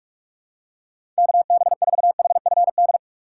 Morse code, heard as a keyed tone, about 44 wpm, 700 Hz. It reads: KB4HFD
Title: USA Call Signs - 6 Characters - 4x Character Spacing - 45wpm